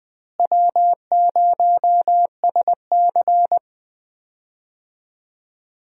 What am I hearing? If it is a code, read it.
Morse code, heard as a keyed tone, about 20 wpm, 700 Hz. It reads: W0SC
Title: USA Call Signs - 4 Characters - Repeated 3x - 20wpm